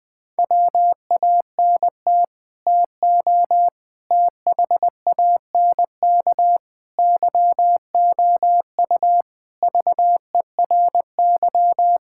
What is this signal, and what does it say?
Morse code, heard as a keyed tone, about 20 wpm, 700 Hz. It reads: WANT TO THANK YOU VERY
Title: Sets of 9 Words - 20wpm